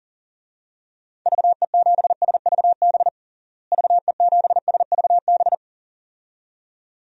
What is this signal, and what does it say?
Morse code, heard as a keyed tone, about 40 wpm, 700 Hz. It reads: VE7SVB VE7SVB
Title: DXCC-100 Call Signs - 6 Characters - Repeated 2x - 40wpm